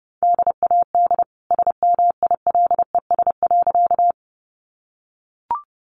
Morse code, heard as a keyed tone, about 30 wpm, 700 Hz. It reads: DAD SMILES.
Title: Easy Phrases 30wpm